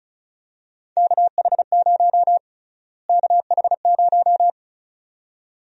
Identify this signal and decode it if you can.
Morse code, heard as a keyed tone, about 34 wpm, 700 Hz. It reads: KH0 KH0
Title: USA Call Sign Prefixes - Repeated 2x - 35wpm